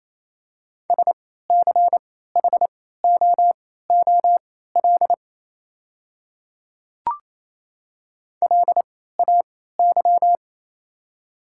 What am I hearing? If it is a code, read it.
Morse code, heard as a keyed tone, about 28 wpm, spaced as for 16 wpm, 700 Hz. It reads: SCHOOL LAY
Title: 3x Character Spacing - Top 500 Words - 28wpm